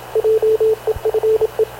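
Morse code, heard as a keyed tone, about 27 wpm, 440 Hz, with radio noise. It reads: JEFE
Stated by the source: Havana Radio / CLA sending Spanish press (PX) in Morse, recorded by Radio Officer Paul Dumesnil onboard CNAV QUEST/CZDO.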